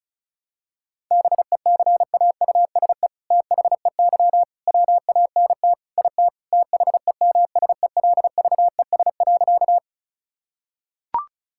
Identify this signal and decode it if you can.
Morse code, heard as a keyed tone, about 34 wpm, 700 Hz. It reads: BECAUSE THEY WANT IT THEMSELVES.
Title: Sentences from Top 1000 Words - Review Part 5 - 35wpm